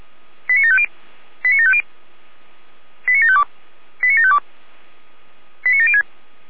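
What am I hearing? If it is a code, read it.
Five-tone selective call (ZVEI), heard as quick runs of five tones, tones of about 80 ms. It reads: 89750 89750 89752 89752 89706
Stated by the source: ZVEI five-tone selective call 89456, received by SDR.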